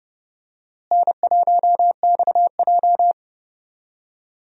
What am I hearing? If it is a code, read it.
Morse code, heard as a keyed tone, about 30 wpm, 700 Hz. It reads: N1XJ
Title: USA Call Signs - 4 Characters - Repeated 2x - 30wpm